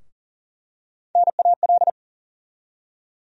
Morse code, heard as a keyed tone, about 38 wpm, 700 Hz. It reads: NAL
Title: USA Call Sign Postfixes - 3 Letters 40wpm